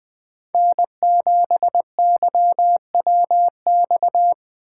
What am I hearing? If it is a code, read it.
Morse code, heard as a keyed tone, about 24 wpm, 700 Hz. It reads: N7YWX
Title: Speed Racing - USA 1x3 Call Signs 20wpm